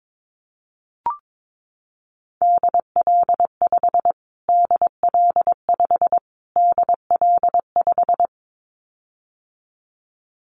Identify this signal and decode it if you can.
Morse code, heard as a keyed tone, about 22 wpm, 700 Hz. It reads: DL5 DL5 DL5
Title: Callsign Prefixes for Top 25 DXCC - Repeated 3x - 22wpm